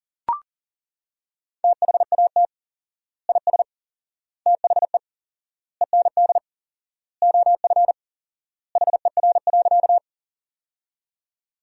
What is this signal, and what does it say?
Morse code, heard as a keyed tone, about 37 wpm, 700 Hz. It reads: THAT IS THE END OF HER.
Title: 4x Word-Spacing - Sentences from Top 200 Words - 40wpm